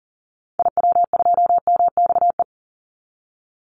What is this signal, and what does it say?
Morse code, heard as a keyed tone, about 40 wpm, 700 Hz. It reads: IW2MXE
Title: Call Signs Repeated 2x - 40wpm